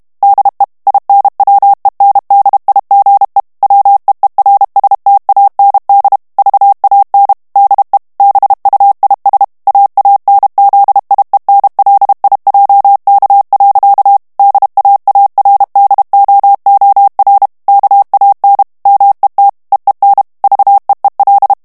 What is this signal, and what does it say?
Morse code, heard as a keyed tone, about 32 wpm, 800 Hz. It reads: DE INWENDIGE WEERSTAND VAN DE BUIS AANZIENLIJK. DAARDOOR KAN MET EEN VEEL
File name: HOLLANDAIS texte clair 2 vitesse 30 mots minute